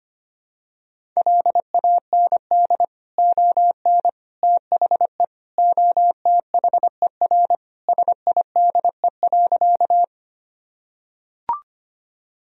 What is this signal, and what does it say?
Morse code, heard as a keyed tone, about 25 wpm, 700 Hz. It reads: LAND ON THE OTHER SIDE.
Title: Sentences from Top 1000 Words - Review Part 1 - 25wpm